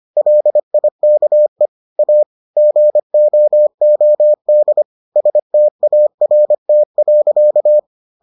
Morse code, heard as a keyed tone, about 25 wpm, 600 Hz. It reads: LIKE A GOOD START.